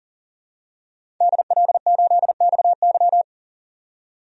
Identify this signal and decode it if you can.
Morse code, heard as a keyed tone, about 40 wpm, 700 Hz. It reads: DL8XY
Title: DXCC-100 Call Signs - 5 Characters - 40wpm